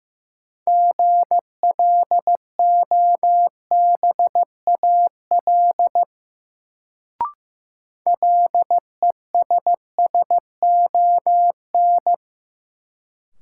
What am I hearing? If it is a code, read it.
Morse code, heard as a keyed tone, about 16 wpm, 700 Hz. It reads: GLOBAL LESSON
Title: Sentences from Top 1200 Words - New Word Drill Alt - 15wpm